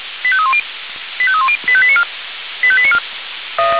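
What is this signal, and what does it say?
Five-tone selective call (ZVEI), heard as quick runs of five tones, tones of about 70 ms. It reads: 85310 85310 85894 85894
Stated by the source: ZVEI five-tone selective call received by SDR, from the Feuerwehr Orlamünde fire brigade audio server.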